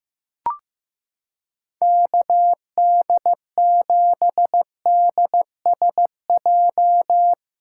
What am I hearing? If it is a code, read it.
Morse code, heard as a keyed tone, about 15 wpm, 700 Hz. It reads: KD7DSJ
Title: USA Callsign DE Callsign 15wpm